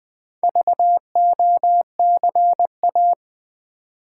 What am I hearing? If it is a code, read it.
Morse code, heard as a keyed tone, about 20 wpm, 700 Hz. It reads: VOCA
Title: Build A Word - 1 Letter At A Time - 5-Letter Words - 20wpm